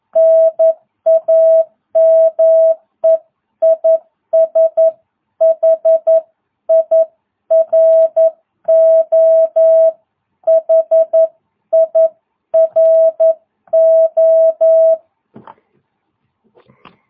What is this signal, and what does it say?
Morse code, heard as a keyed tone, about 11 wpm, 660 Hz. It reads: NAME IS HIRO HIRO